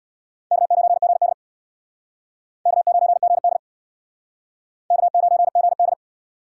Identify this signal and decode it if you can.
Morse code, heard as a keyed tone, about 64 wpm, 700 Hz. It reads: K9GD K9GD K9GD